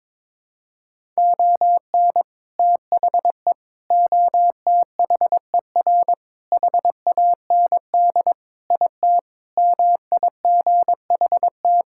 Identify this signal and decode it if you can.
Morse code, heard as a keyed tone, about 22 wpm, 700 Hz. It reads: ON THE OTHER HAND IT MIGHT
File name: practice / 22wpm / Sentences with 3-Word Phrases - 22wpm